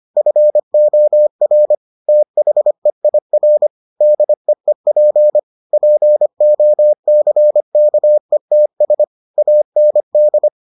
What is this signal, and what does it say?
Morse code, heard as a keyed tone, about 25 wpm, 600 Hz. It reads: FOR THEIR DEEP POCKETS AND